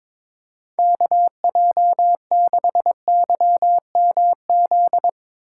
Morse code, heard as a keyed tone, about 22 wpm, 700 Hz. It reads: KJ6YMZ
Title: USA Call Signs - 6 Characters - 4x Character Spacing - 22wpm